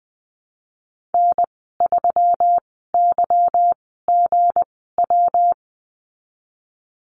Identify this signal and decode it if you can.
Morse code, heard as a keyed tone, about 20 wpm, spaced as for 15 wpm, 700 Hz. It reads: N3YGW
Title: 186 Lesson 30g - Letter b - Call Signs 20wpm